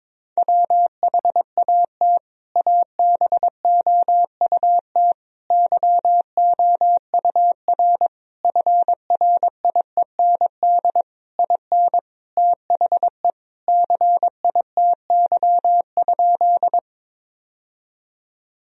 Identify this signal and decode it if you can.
Morse code, heard as a keyed tone, about 22 wpm, 700 Hz. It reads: WHAT ABOUT YOUR FRIEND IN THE CITY?
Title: Sentences from Top 1000 Words - Review Part 1 - 22wpm